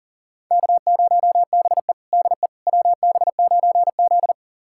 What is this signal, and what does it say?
Morse code, heard as a keyed tone, about 37 wpm, 700 Hz. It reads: K0BE DE WB9Z
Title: USA Callsign DE Callsign - 4 Characters - 40wpm